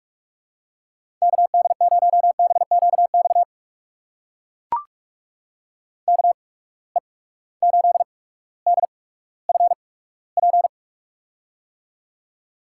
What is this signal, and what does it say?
Morse code, heard as a keyed tone, about 43 wpm, spaced as for 19 wpm, 700 Hz. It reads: KD0BQX KE8DFP
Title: USA Call Signs - 6 Characters - 8x Character Spacing - 45wpm